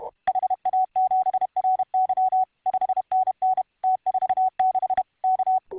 Morse code, heard as a keyed tone, about 31 wpm, 750 Hz. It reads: HA7RY 5NN T46 K